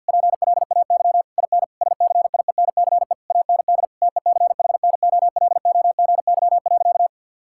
Morse code, heard as a keyed tone, about 53 wpm, 700 Hz. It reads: PLAY IN SCIENCE AND TECHNOLOGY.
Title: Sentences from Top 1000 Words - Review Part 6 - 50wpm